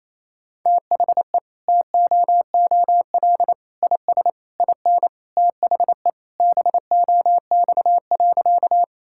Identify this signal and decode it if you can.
Morse code, heard as a keyed tone, about 28 wpm, 700 Hz. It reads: THE TOOL IS IN THE BOX.
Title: Sentences from Top 1000 Words - Review Part 6 - 28wpm